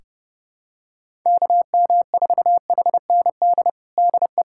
Morse code, heard as a keyed tone, about 29 wpm, 700 Hz. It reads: KM4HND DE
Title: USA Callsign DE Callsign 30wpm